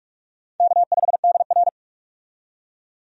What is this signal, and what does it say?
Morse code, heard as a keyed tone, about 46 wpm, 700 Hz. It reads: K5DR
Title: USA Call Signs - 4 Characters - 45wpm